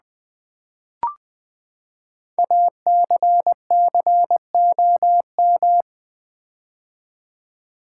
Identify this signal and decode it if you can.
Morse code, heard as a keyed tone, about 20 wpm, 700 Hz. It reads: ACCOM
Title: Build A Word - 5 Letters At A Time - 10-Letter Words - 20wpm